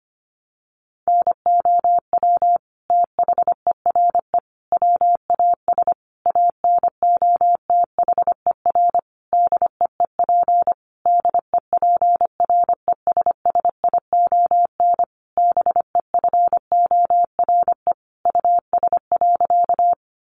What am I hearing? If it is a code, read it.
Morse code, heard as a keyed tone, about 25 wpm, 700 Hz. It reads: NOW THERE WAS ANOTHER DEEP DEPRESSION BEFORE US.
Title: Sentences from Top 1300 Words 25wpm